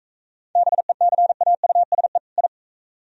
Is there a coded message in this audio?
Yes: Morse code from 0.6 to 2.5 s, heard as a keyed tone, about 41 wpm, 690 Hz. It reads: BECAUSE I